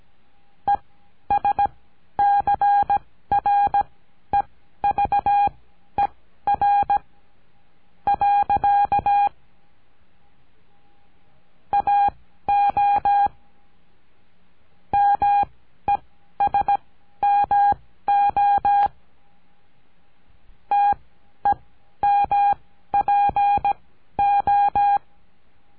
Morse code, heard as a keyed tone, about 17 wpm, spaced as for 11 wpm, 810 Hz. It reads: ESCREVER. AO MESMO TEMPO
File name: CURSO DE TELEGRAFIA